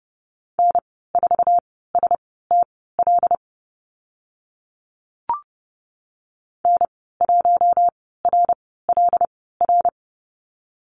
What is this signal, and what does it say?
Morse code, heard as a keyed tone, about 30 wpm, spaced as for 17 wpm, 700 Hz. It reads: N4STL N1RLR
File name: Lesson 14g - Number 2 - Call Signs 30wpm